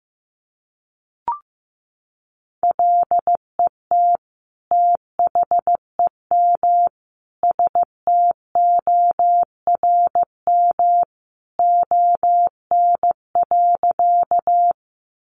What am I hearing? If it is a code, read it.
Morse code, heard as a keyed tone, about 15 wpm, 700 Hz. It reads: LET THEM STORM ON.
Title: Sentences from Top 1500 Words 15wpm